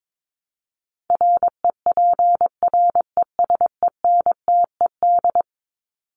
Morse code, heard as a keyed tone, about 22 wpm, 700 Hz. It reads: REPRESENTED
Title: ED Suffix - Top 100 Words with Base Words 22wpm